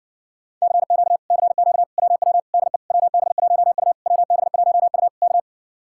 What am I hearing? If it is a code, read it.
Morse code, heard as a keyed tone, about 60 wpm, 700 Hz. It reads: CQ CQ WW DE WB1U WB1U K